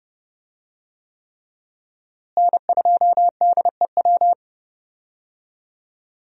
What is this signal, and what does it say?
Morse code, heard as a keyed tone, about 30 wpm, 700 Hz. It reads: N2DEW
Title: USA Call Signs - 5 Characters - Repeated 3x - 30wpm